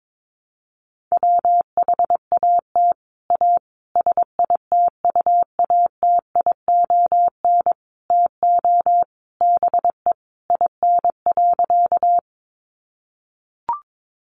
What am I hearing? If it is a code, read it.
Morse code, heard as a keyed tone, about 22 wpm, 700 Hz. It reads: WHAT A SITUATION TO BE IN.
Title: Sentences from Top 1200 Words 22wpm